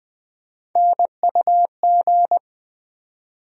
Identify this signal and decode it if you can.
Morse code, heard as a keyed tone, about 20 wpm, 700 Hz. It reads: NUG
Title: USA Call Sign Postfixes - 3 Letters 20wpm